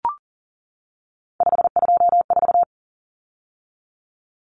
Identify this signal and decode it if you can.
Morse code, heard as a keyed tone, about 40 wpm, 700 Hz. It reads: 524
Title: Common Cut Numbers - 3-Digit 40wpm